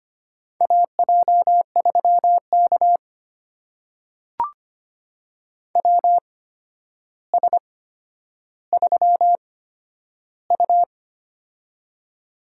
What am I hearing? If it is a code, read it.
Morse code, heard as a keyed tone, about 25 wpm, spaced as for 11 wpm, 700 Hz. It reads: AJ3K WS3U
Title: USA Call Signs - 4 Characters - 8x Character Spacing - 25wpm